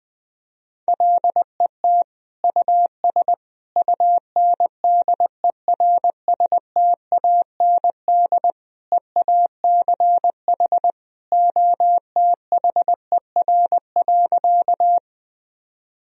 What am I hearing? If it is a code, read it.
Morse code, heard as a keyed tone, about 21 wpm, 700 Hz. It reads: LET US UNDERSTAND EACH OTHER.
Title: Sentences from Top 1000 Words - Review Part 1 - 20wpm